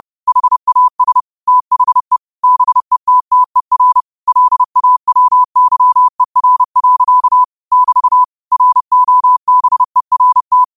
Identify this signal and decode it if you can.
Morse code, heard as a keyed tone, about 30 wpm, 1000 Hz. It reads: HAS THE BETTER LAWYER. <BT> ROBERT
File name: qotdfiles / 30wpm